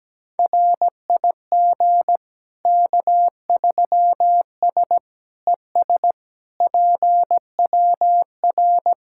Morse code, heard as a keyed tone, about 17 wpm, 700 Hz. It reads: RIG K3S ES PWR
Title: Practice QSO - Exchange 2 - Info - 17wpm